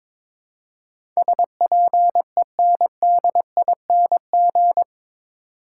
Morse code, heard as a keyed tone, about 22 wpm, 700 Hz. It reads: SPENDING
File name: practice / 22wpm / Eight Letter Words - 22wpm